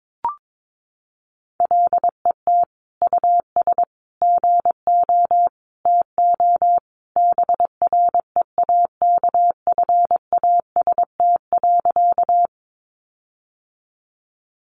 Morse code, heard as a keyed tone, about 22 wpm, 700 Hz. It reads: LET US GO TO BREAKFAST.
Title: Sentences from Top 1500 Words 22wpm